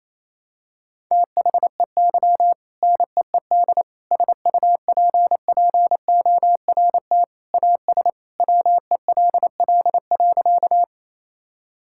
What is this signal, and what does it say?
Morse code, heard as a keyed tone, about 28 wpm, 700 Hz. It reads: THEY NEED SUPPORT AS WELL.
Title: Sentences from Top 1000 Words - Review Part 5 - 28wpm